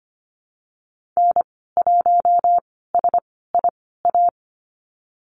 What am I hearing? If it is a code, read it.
Morse code, heard as a keyed tone, about 25 wpm, spaced as for 16 wpm, 700 Hz. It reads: N1SIA
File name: Lesson 15g - Number 5 - Call Signs 25wpm